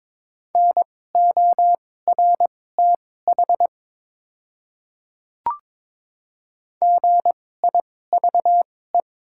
Morse code, heard as a keyed tone, about 21 wpm, spaced as for 16 wpm, 700 Hz. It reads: NORTH GIVE
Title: 2x Character Spacing - Top 300 Words - 22wpm